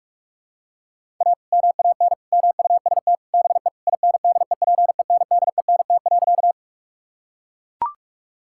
Morse code, heard as a keyed tone, about 41 wpm, 700 Hz. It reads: A MAN MUST BE INDEPENDENT.
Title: Sentences from Top 1000 Words - Review Part 6 - 45wpm